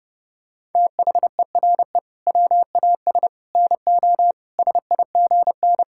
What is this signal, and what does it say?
Morse code, heard as a keyed tone, about 28 wpm, 700 Hz. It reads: THERE WAS NO SIGN
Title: Sentences from Top 1000 Words - Review Part 6 - 30wpm